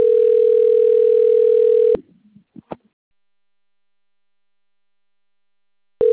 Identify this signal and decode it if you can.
Ringback tone (North America), 440 + 480 Hz.